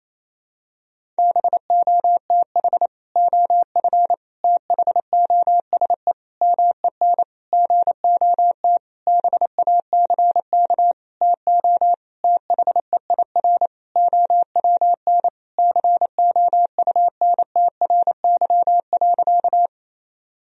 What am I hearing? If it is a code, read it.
Morse code, heard as a keyed tone, about 27 wpm, 700 Hz. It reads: BOTH OF THOSE MEN GOT BACK TO THEIR OWN COUNTRY.
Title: Sentences from Top 1000 Words - Review Part 1 - 28wpm